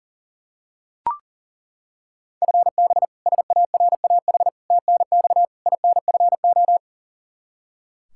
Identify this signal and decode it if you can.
Morse code, heard as a keyed tone, about 37 wpm, 700 Hz. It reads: FB SARAH TNX INFO
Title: K1USN SST QSO Practice 40wpm